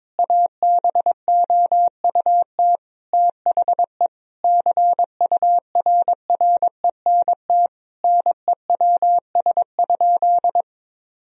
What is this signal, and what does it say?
Morse code, heard as a keyed tone, about 22 wpm, 700 Hz. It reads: ABOUT THE CURRENT NEWS?
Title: Sentences from Top 1000 Words - Review Part 4 - 22wpm